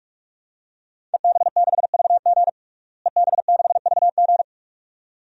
Morse code, heard as a keyed tone, about 46 wpm, 700 Hz. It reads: EB6VG EB6VG
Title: DXCC-100 Call Signs - 5 Characters - Repeated 2x - 45wpm